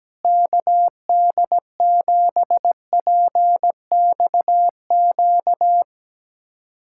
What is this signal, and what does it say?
Morse code, heard as a keyed tone, about 17 wpm, 700 Hz. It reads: KD7PXQ
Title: USA Callsign DE Callsign - 6 Characters - 17wpm